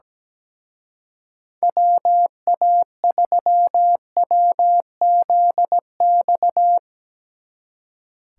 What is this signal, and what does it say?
Morse code, heard as a keyed tone, about 19 wpm, 700 Hz. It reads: WA3WZX
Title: Speed Racing - USA 2x3 Call Signs 17wpm